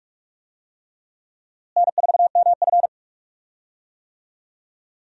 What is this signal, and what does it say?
Morse code, heard as a keyed tone, about 41 wpm, 700 Hz. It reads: N4MF
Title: USA Call Signs - 4 Characters - Repeated 3x - 45wpm